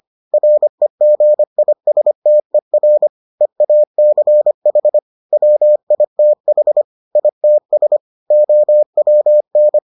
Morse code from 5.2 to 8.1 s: WITH ITS

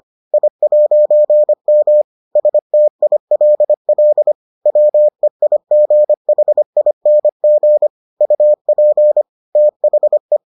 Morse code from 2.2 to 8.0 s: STILL WEIGHING

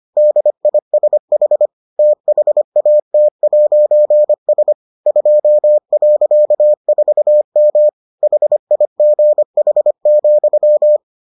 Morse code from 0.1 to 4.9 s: DISH THAT'S

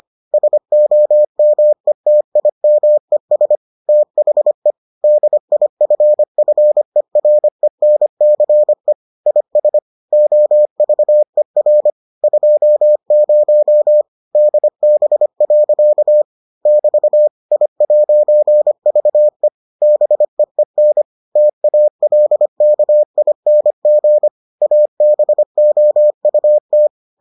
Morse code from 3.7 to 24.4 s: THE DIFFERENCE IS OVER 20 DB. <BT> I'VE BEEN TALKING

